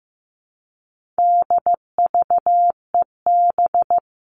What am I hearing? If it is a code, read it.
Morse code, heard as a keyed tone, about 15 wpm, 700 Hz. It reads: DVEB
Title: All Cut Numbers - 4-Digit 15wpm